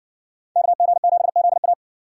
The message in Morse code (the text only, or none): KG7ZA